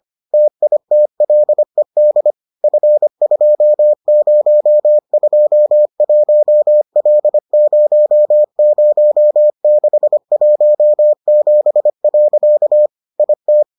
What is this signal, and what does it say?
Morse code, heard as a keyed tone, about 25 wpm, 600 Hz. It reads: TITLED F2021L00617. IT